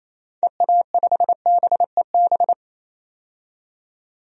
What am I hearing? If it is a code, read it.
Morse code, heard as a keyed tone, about 28 wpm, 700 Hz. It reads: EA5BEB
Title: DXCC-100 Call Signs - 6 Characters - 28wpm